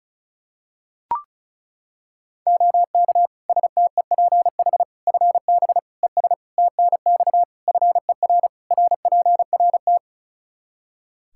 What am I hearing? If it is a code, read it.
Morse code, heard as a keyed tone, about 36 wpm, 700 Hz. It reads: OK STEPH FB ES TNX FER RPRT
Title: Practice QSO - Exchange 2 - Info - 35wpm